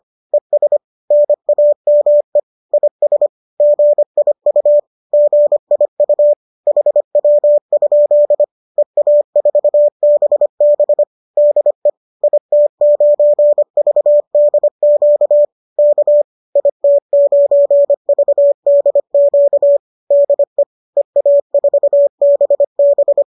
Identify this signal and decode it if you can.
Morse code, heard as a keyed tone, about 25 wpm, 590 Hz. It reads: ES NAME IS GIU GIU HW? EA4BB DE IT9VDQ K IT9VDQ DE EA4BB